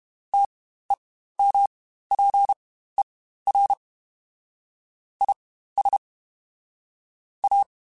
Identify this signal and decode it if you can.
Morse code, heard as a keyed tone, about 32 wpm, spaced as for 15 wpm, 800 Hz. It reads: TEMPER IS A
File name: Quotes Spaced / Quotes Spaced-mp3s / Temper sp @30